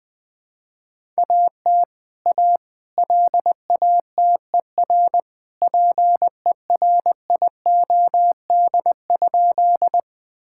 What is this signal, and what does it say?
Morse code, heard as a keyed tone, about 20 wpm, 700 Hz. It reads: AT A LATER PERIOD?
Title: Sentences from Top 1000 Words - Review Part 5 - 20wpm